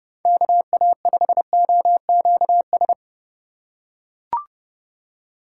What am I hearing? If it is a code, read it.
Morse code, heard as a keyed tone, about 31 wpm, 700 Hz. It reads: KA5OQS